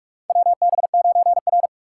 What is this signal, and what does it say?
Morse code, heard as a keyed tone, about 47 wpm, 700 Hz. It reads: WB9R
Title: USA Call Signs - 3x Character Spacing - 45wpm